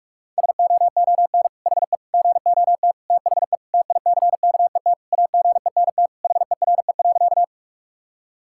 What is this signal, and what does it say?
Morse code, heard as a keyed tone, about 46 wpm, 700 Hz. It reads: SOON HE GOT THE TICKET AGENT HERE.
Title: Sentences from Top 1600 Words 45wpm